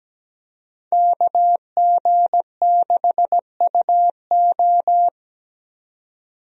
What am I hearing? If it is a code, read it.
Morse code, heard as a keyed tone, about 17 wpm, 700 Hz. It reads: KG6UO
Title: USA Call Signs - 5 Characters - 17wpm